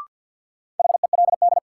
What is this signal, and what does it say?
Morse code, heard as a keyed tone, about 48 wpm, 700 Hz. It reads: HELD
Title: Sentences from Top 1600 Words - New Word Drill - 50wpm